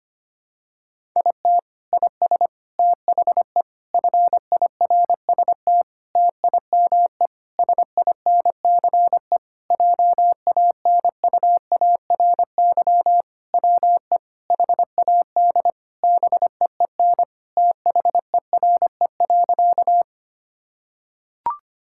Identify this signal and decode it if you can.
Morse code, heard as a keyed tone, about 25 wpm, 700 Hz. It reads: IT IS THE FIRST TIME SINCE JANUARY WE HAD BEEN THERE.
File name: practice / 25wpm / Sentences with 4-Word Phrases - 25wpm